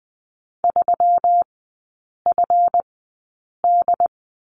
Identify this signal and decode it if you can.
Morse code, heard as a keyed tone, about 20 wpm, spaced as for 8 wpm, 700 Hz. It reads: 3FD
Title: 166 Lesson 27e - Number 7 - Three Characters 20wpm